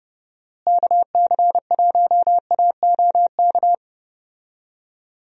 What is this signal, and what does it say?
Morse code, heard as a keyed tone, about 30 wpm, 700 Hz. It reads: KC1AOK
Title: USA Call Signs - 6 Characters - Repeated 2x - 30wpm